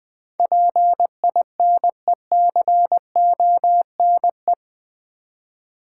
Morse code, heard as a keyed tone, about 20 wpm, 690 Hz. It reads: PINECONE
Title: Difficult Words with Hints - 20wpm